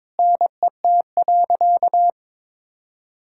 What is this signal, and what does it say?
Morse code, heard as a keyed tone, about 22 wpm, 700 Hz. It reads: NET.
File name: Sentences with 9-Word Phrases - 22wpm